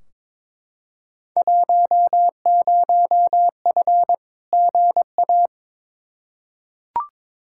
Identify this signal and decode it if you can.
Morse code, heard as a keyed tone, about 22 wpm, 700 Hz. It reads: 10F GA
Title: Field Day Exchanges Encore - 22wpm